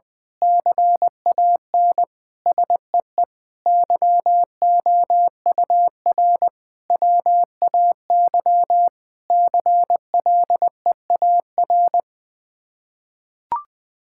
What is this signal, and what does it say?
Morse code, heard as a keyed tone, about 21 wpm, 700 Hz. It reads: CAN SEE YOUR WAY CLEAR
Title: Sets of 8 Words - 20wpm